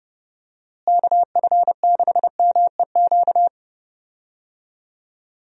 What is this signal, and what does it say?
Morse code, heard as a keyed tone, about 31 wpm, 700 Hz. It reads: KF6MEQ
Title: USA Call Signs - 6 Characters - Repeated 2x - 30wpm